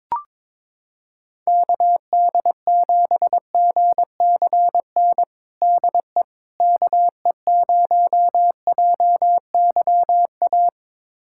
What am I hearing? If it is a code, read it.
Morse code, heard as a keyed tone, about 22 wpm, 700 Hz. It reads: KD7GCN DE KE0JYA
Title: USA Callsign DE Callsign 22wpm